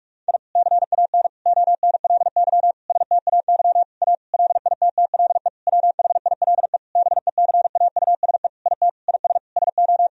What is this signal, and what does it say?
Morse code, heard as a keyed tone, about 45 wpm, 700 Hz. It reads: I CAN ONLY STAY A LITTLE WHILE BECAUSE IT IS SO
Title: Sentences from Top 1000 Words - Review Part 3 - 45wpm